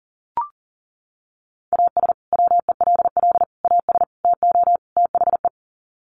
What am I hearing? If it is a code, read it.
Morse code, heard as a keyed tone, about 40 wpm, 700 Hz. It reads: AS WELL AS TO THE
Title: Sets of 5 Words Encore2 40wpm